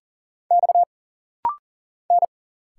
Morse code, heard as a keyed tone, about 40 wpm, 700 Hz. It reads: X N
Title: Single Letters - Rapid-Fire 40wpm